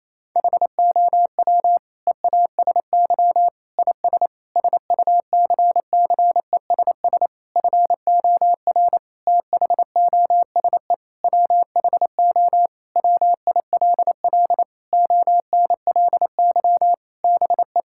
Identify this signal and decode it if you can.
Morse code, heard as a keyed tone, about 28 wpm, 700 Hz. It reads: HOW EASY IS SUCCESS FOR THOSE WHO WILL ONLY BE
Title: Sentences from Top 1000 Words - Review Part 6 - 28wpm